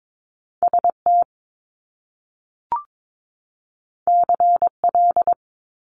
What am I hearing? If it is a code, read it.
Morse code, heard as a keyed tone, about 22 wpm, 700 Hz. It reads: ST CL
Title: Top Two Letter Combinations in Words 22wpm